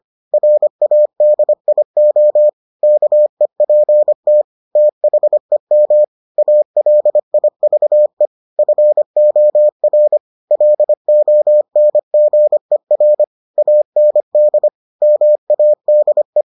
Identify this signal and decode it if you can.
Morse code, heard as a keyed tone, about 25 wpm, 600 Hz. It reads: RADIO KEPT THEM ALIVE FOR LONGER AND MADE